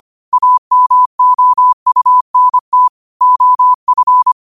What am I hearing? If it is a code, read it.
Morse code, heard as a keyed tone, about 25 wpm, 1000 Hz. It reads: AMOUNT OF